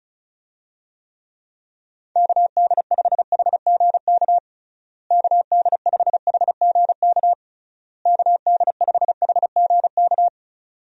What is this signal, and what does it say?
Morse code, heard as a keyed tone, about 34 wpm, 700 Hz. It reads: KD5HGK KD5HGK KD5HGK
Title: USA Call Signs - 6 Characters - Repeated 3x - 35wpm